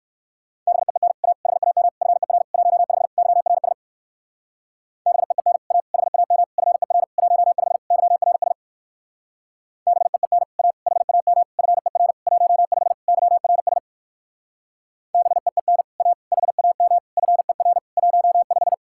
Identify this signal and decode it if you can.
Morse code, heard as a keyed tone, about 64 wpm, 700 Hz. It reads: BEEN A HAM FER 15 YRS BEEN A HAM FER 15 YRS BEEN A HAM FER 15 YRS BEEN A HAM FER 15